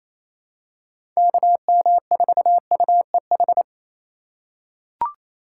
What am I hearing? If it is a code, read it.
Morse code, heard as a keyed tone, about 28 wpm, 700 Hz. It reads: KM4UEH
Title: USA Call Signs - 4x Character Spacing - 28wpm